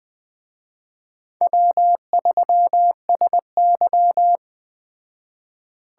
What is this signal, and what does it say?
Morse code, heard as a keyed tone, about 20 wpm, 700 Hz. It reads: W3SY
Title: Speed Racing - USA 1x2 Call Signs 17wpm